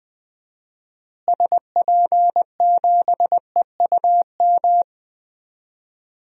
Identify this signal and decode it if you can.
Morse code, heard as a keyed tone, about 20 wpm, 700 Hz. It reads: SP7EUM
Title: DXCC-100 Call Signs - 6 Characters - Repeated 2x - 20wpm